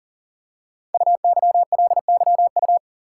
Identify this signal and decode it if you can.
Morse code, heard as a keyed tone, about 39 wpm, 700 Hz. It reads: UYLYU
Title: Code Groups - 5 Letters - 40wpm